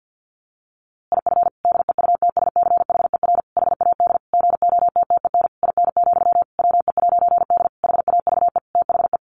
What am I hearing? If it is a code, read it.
Morse code, heard as a keyed tone, about 50 wpm, 700 Hz. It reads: IF DEUTSCHER HAD GOTTEN IN, WE'D HAVE THE
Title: A Sound Of Thunder 50wpm